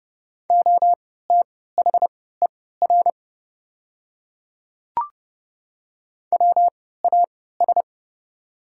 Morse code, heard as a keyed tone, about 30 wpm, spaced as for 17 wpm, 700 Hz. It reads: OTHER WAS
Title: 3x Character Spacing - Top 100 Words - 30wpm